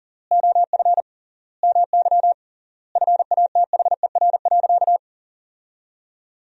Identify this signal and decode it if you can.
Morse code, heard as a keyed tone, about 40 wpm, 700 Hz. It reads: OF MY FATHER.
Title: 3x Word-Spacing - Sentences from Top 500 Words - 40wpm